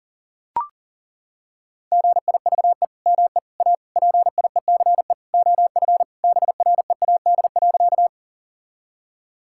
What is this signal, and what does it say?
Morse code, heard as a keyed tone, about 40 wpm, 700 Hz. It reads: GIVE ME A PIECE OF BREAD.
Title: Sentences from Top 1600 Words 40wpm